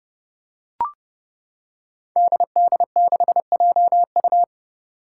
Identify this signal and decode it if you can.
Morse code, heard as a keyed tone, about 32 wpm, 700 Hz. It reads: DD6JU
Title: Five Challenging Letter-Number Combinations 30wpm